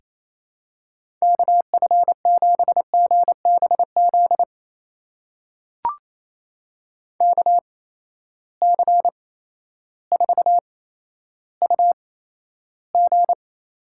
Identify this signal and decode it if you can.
Morse code, heard as a keyed tone, about 28 wpm, spaced as for 13 wpm, 700 Hz. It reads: KF7GBZ KC4UG
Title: USA Call Signs - 6 Characters - 8x Character Spacing - 28wpm